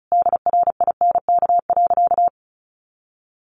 Morse code, heard as a keyed tone, about 34 wpm, 700 Hz. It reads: DRINK.
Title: Sentences from Top 1500 Words 35wpm